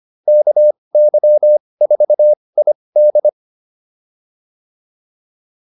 Morse code, heard as a keyed tone, about 25 wpm, spaced as for 20 wpm, 600 Hz. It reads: KY4ID